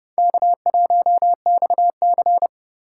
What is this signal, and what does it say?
Morse code, heard as a keyed tone, about 28 wpm, 700 Hz. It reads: K1XC